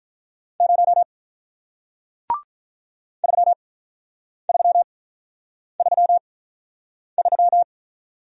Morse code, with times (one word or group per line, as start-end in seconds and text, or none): 0.6-1.0 0
3.2-7.6 3333